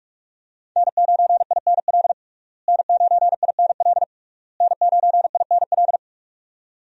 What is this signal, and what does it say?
Morse code, heard as a keyed tone, about 41 wpm, 700 Hz. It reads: N9INL N9INL N9INL
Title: USA Call Signs - 5 Characters - Repeated 3x - 45wpm